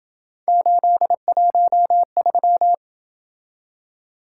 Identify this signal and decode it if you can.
Morse code, heard as a keyed tone, about 27 wpm, 700 Hz. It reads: 813